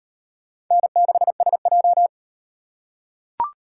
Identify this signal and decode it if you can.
Morse code, heard as a keyed tone, about 36 wpm, 700 Hz. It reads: N6SJ